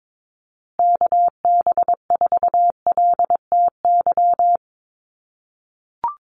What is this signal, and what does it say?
Morse code, heard as a keyed tone, about 22 wpm, 700 Hz. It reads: KB4LTY
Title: USA Call Signs Repeated 2x 22wpm